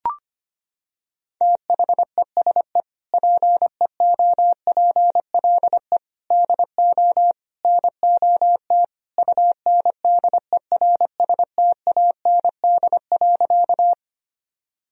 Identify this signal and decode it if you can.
Morse code, heard as a keyed tone, about 25 wpm, 700 Hz. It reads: THESE PEOPLE DO NOT UNDERSTAND.
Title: Sentences from Top 1000 Words - Review Part 1 - 25wpm